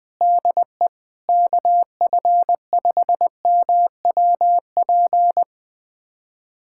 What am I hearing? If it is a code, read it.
Morse code, heard as a keyed tone, about 20 wpm, 700 Hz. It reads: DE KF5MWP
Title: USA Callsign DE Callsign 20wpm